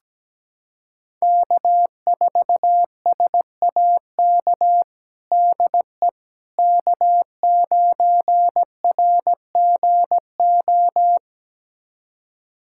Morse code, heard as a keyed tone, about 17 wpm, 700 Hz. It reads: K4SAK DE K9RGO
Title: USA Callsign DE Callsign - 5 Characters - 17wpm